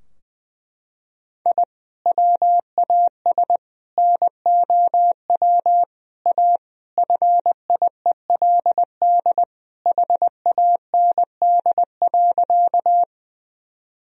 Morse code, heard as a keyed tone, about 20 wpm, 700 Hz. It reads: I WAS NOW A FIELD HAND.
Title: Sentences from Top 1000 Words - Review Part 2 - 20wpm